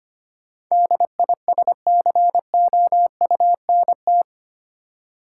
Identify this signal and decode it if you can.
Morse code, heard as a keyed tone, about 25 wpm, 700 Hz. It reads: DISCOUNT
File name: 8 Letter Words - First 3 Letters Given - 25wpm